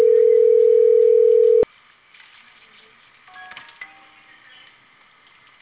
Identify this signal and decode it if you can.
Ringback tone (North America), 440 + 480 Hz.